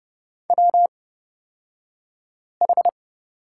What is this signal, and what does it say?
Morse code, heard as a keyed tone, about 30 wpm, spaced as for 5 wpm, 700 Hz. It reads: WH
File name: Lesson 38d - Letter q - Two Characters 30wpm